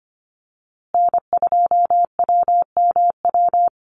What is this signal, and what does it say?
Morse code, heard as a keyed tone, about 25 wpm, 700 Hz. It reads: N2WMW
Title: USA Call Signs Repeated 2x 25wpm